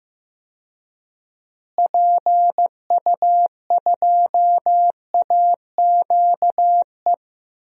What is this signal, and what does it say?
Morse code, heard as a keyed tone, about 15 wpm, 700 Hz. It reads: PU2AQE
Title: DXCC-100 Call Signs - 6 Characters - Repeated 2x - 15wpm